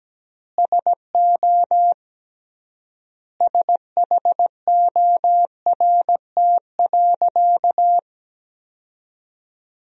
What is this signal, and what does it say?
Morse code, heard as a keyed tone, about 17 wpm, 700 Hz. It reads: SO SHORT.
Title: 3x Word-Spacing - Sentences from Top 500 Words - 17wpm